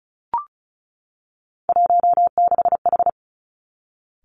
Morse code, heard as a keyed tone, about 36 wpm, 700 Hz. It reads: 16H
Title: Three Challenging Letter-Number Combinations 35wpm